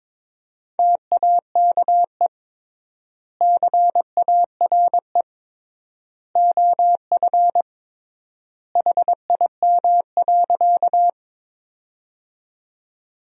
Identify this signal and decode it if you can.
Morse code, heard as a keyed tone, about 21 wpm, 700 Hz. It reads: TAKE CARE OF HIM.
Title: 3x Word-Spacing - Sentences from Top 300 Words - 22wpm